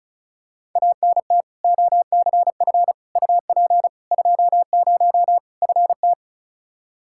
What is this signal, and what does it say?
Morse code, heard as a keyed tone, about 36 wpm, 700 Hz. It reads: ANT OCF UP 20 FT
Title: K1USN SST QSO Practice 35wpm